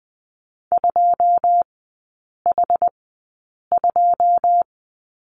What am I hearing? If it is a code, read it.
Morse code, heard as a keyed tone, about 20 wpm, spaced as for 8 wpm, 690 Hz. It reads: 2H2
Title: 081 Lesson 14e - Number 2 - Three Characters 20wpm